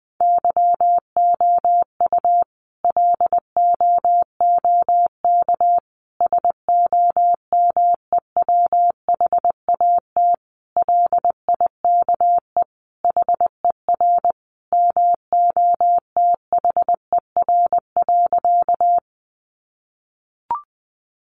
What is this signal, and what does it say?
Morse code, heard as a keyed tone, about 19 wpm, 700 Hz. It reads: YOU LOOK SOMEWHAT LIKE HER MOTHER.
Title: Sentences from Top 1300 Words 20wpm